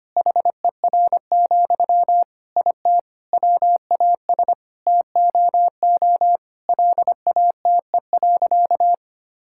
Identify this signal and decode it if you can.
Morse code, heard as a keyed tone, about 25 wpm, 700 Hz. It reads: HER, IT WAS TOO LATE.
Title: Sentences from Top 1000 Words - Review Part 6 - 25wpm